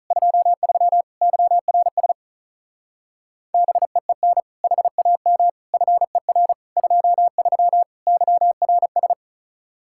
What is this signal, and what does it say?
Morse code, heard as a keyed tone, about 43 wpm, 700 Hz. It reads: 23 YRS BEEN HAM FER 23 YRS
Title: Speed-Racing Common QSO Phrases - 35wpm